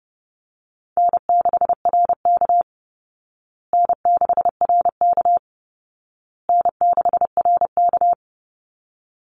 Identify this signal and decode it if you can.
Morse code, heard as a keyed tone, about 30 wpm, 700 Hz. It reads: N6RK N6RK N6RK
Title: Easy Call Signs Repeated 3x - 30wpm